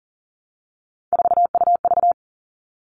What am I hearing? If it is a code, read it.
Morse code, heard as a keyed tone, about 40 wpm, 700 Hz. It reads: 4UV
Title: All Cut Numbers - 3-Digit 40wpm